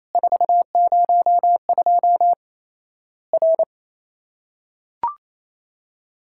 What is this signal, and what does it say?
Morse code, heard as a keyed tone, about 28 wpm, 700 Hz. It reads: 402 R
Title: Worked All Europe DX Contest - QTC Exchange - 28wpm